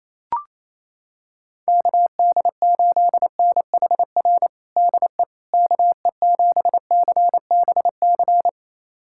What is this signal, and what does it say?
Morse code, heard as a keyed tone, about 28 wpm, 700 Hz. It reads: KD8NHR DE KE7CBC
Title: USA Callsign DE Callsign 28wpm